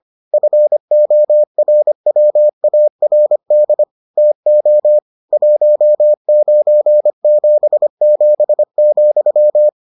Morse code, heard as a keyed tone, about 25 wpm, 600 Hz. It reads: FORWARD TO 1977,